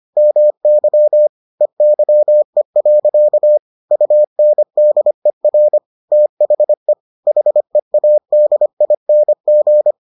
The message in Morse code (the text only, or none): MY EYE. UNDER THE HEADING